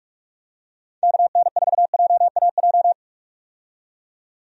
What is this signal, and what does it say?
Morse code, heard as a keyed tone, about 46 wpm, 700 Hz. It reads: KN4JAJ
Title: USA Call Signs - 6 Characters - 45wpm